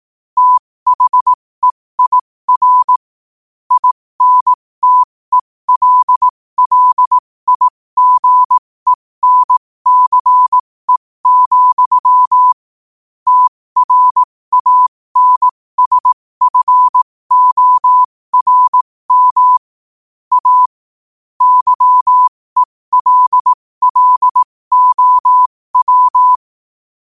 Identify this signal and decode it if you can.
Morse code, heard as a keyed tone, about 18 wpm, spaced as for 15 wpm, 1000 Hz. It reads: THEIR INTELLIGENCE, TRANSFORM A YELLOW
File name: qotdfiles / 15wpm